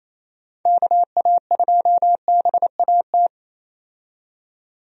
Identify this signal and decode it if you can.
Morse code, heard as a keyed tone, about 28 wpm, 700 Hz. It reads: KA2BAT
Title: USA Call Signs - 6 Characters - 28wpm